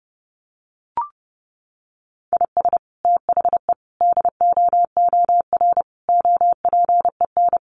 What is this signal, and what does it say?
Morse code, heard as a keyed tone, about 28 wpm, 700 Hz. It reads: IS THE DOOR OPEN
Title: Sentences from Top 1300 Words 30wpm